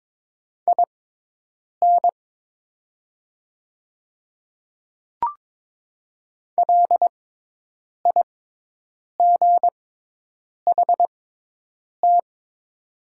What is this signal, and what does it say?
Morse code, heard as a keyed tone, about 22 wpm, spaced as for 8 wpm, 700 Hz. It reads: IN LIGHT
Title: 6x Character Spacing - Top 200 Words - 22wpm